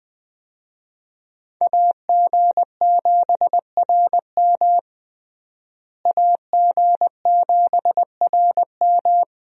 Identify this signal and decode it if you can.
Morse code, heard as a keyed tone, about 20 wpm, 700 Hz. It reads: AG7RM AG7RM
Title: USA Call Signs - 5 Characters - Repeated 3x - 20wpm